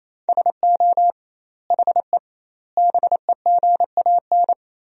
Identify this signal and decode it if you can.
Morse code, heard as a keyed tone, about 28 wpm, 700 Hz. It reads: SO HE BEGAN
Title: 2x Word-Spacing - Sentences from Top 300 Words - 28wpm